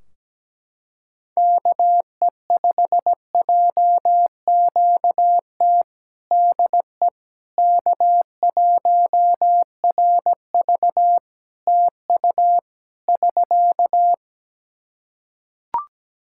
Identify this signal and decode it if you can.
Morse code, heard as a keyed tone, about 17 wpm, 700 Hz. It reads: KE5JQT DE K1RV TU <SK>
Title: Practice QSO - Exchange 4 - Ending - 17wpm